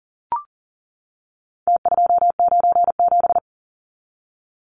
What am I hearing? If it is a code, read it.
Morse code, heard as a keyed tone, about 40 wpm, 700 Hz. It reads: T297
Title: Numbers - Contest Style - 4-Digit 40wpm